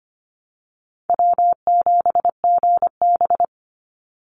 Morse code, heard as a keyed tone, about 25 wpm, 700 Hz. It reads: W7GB
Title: Easy Call Signs Repeated 3x - 25wpm